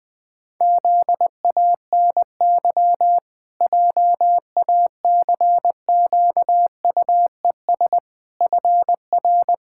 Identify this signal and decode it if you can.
Morse code, heard as a keyed tone, about 20 wpm, 700 Hz. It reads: ZANY JACQUES FR
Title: Pangram Challenge v2 - 20wpm